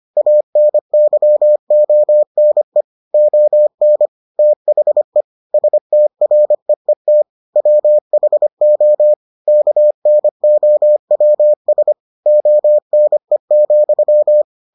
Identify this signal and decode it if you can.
Morse code, heard as a keyed tone, about 25 wpm, 600 Hz. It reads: ANYONE ON THE STREET WHO KNOWS ONE,